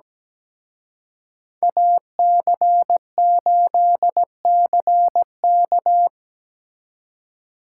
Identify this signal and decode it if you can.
Morse code, heard as a keyed tone, about 20 wpm, 700 Hz. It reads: AC8CK
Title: Speed Racing - USA Call Signs 17wpm